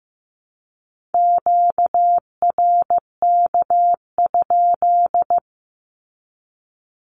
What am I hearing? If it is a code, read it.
Morse code, heard as a keyed tone, about 15 wpm, 700 Hz. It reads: QRK?
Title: Q-Codes with Spoken Meaning - 15wpm